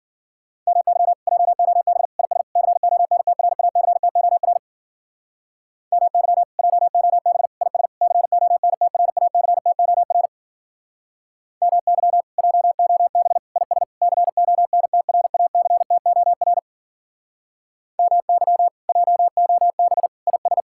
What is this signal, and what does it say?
Morse code, heard as a keyed tone, about 50 wpm, 700 Hz. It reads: MY JOB IS CONTRACTOR MY JOB IS CONTRACTOR MY JOB IS CONTRACTOR MY JOB IS
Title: Speed-Racing Common QSO Phrases - 40wpm